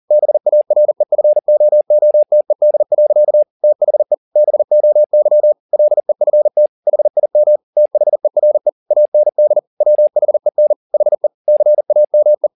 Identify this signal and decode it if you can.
Morse code, heard as a keyed tone, about 40 wpm, 600 Hz. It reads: BAREFOOTED. THE BOY LEFT HIM THERE AND WHEN HE CAME